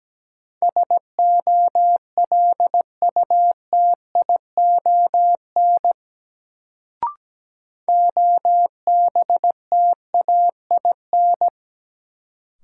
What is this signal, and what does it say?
Morse code, heard as a keyed tone, about 17 wpm, 700 Hz. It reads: SOLUTION OBTAIN
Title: Sentences from Top 1300 Words - New Word Drill - 17wpm